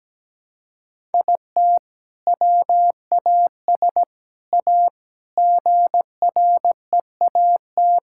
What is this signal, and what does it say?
Morse code, heard as a keyed tone, about 17 wpm, 700 Hz. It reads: IT WAS A GREAT
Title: Sentences from Top 1000 Words - Review Part 6 - 17wpm